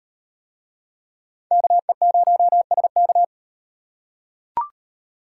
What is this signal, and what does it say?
Morse code, heard as a keyed tone, about 38 wpm, 700 Hz. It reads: KE0SK